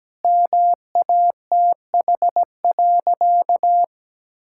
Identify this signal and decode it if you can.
Morse code, heard as a keyed tone, about 17 wpm, 700 Hz. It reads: MATH.
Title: Sentences from Top 1000 Words - Review Part 6 - 17wpm